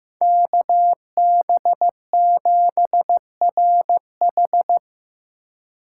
Morse code, heard as a keyed tone, about 15 wpm, 700 Hz. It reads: KB7RH
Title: USA Call Signs - 2x Character Spacing - 15wpm